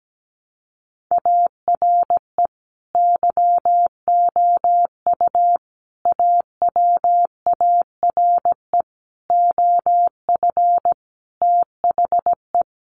Morse code, heard as a keyed tone, about 17 wpm, 700 Hz. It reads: ARE YOU AWARE OF THE
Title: Sentences from Top 1200 Words 17wpm